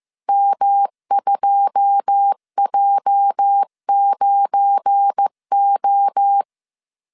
Morse code, heard as a keyed tone, about 15 wpm, 800 Hz. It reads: M2J9O
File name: CW / DET / 1 9 1500C S C 14wpm 3245